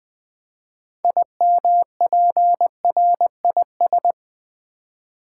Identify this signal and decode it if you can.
Morse code, heard as a keyed tone, about 20 wpm, 700 Hz. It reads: IMPRIS
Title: Build A Word - 3 Letters At A Time - 8-Letter Words - 20wpm